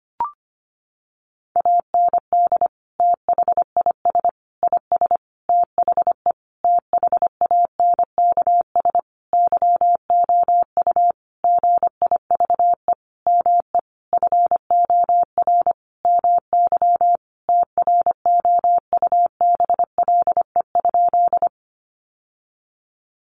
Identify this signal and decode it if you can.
Morse code, heard as a keyed tone, about 25 wpm, 700 Hz. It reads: AND THIS IS THE THANKS YOU GIVE ME FOR MY TROUBLE?
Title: Sentences from Top 1200 Words 25wpm